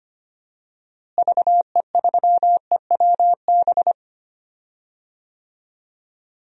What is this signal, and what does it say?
Morse code, heard as a keyed tone, about 25 wpm, 700 Hz. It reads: VE3EWB
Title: DXCC-100 Call Signs - 6 Characters - Repeated 2x - 25wpm